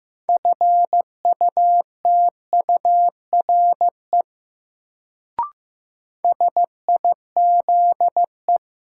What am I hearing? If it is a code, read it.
Morse code, heard as a keyed tone, about 15 wpm, 700 Hz. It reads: FUTURE SIZE
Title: Sets of 3 Words - New Word Drill - 15wpm